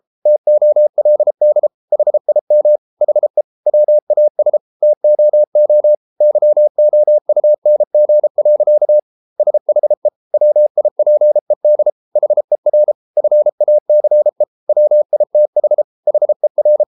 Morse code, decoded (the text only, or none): TOLD HIM HE WAS TOO YOUNG. SHE WIPED HER FACE WITH HER